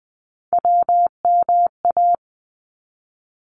Morse code, heard as a keyed tone, about 20 wpm, 700 Hz. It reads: WMA
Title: ARRL Sections 20wpm